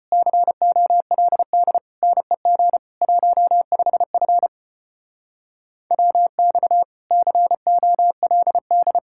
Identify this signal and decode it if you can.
Morse code, heard as a keyed tone, about 32 wpm, 700 Hz. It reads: COLD NEG 15F WX COLD